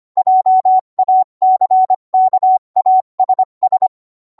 Morse code, heard as a keyed tone, about 25 wpm, 750 Hz. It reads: JACKASS